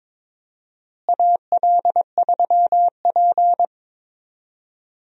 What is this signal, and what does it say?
Morse code, heard as a keyed tone, about 22 wpm, 700 Hz. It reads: AL3P
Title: USA Call Signs - 4 Characters - Repeated 3x - 22wpm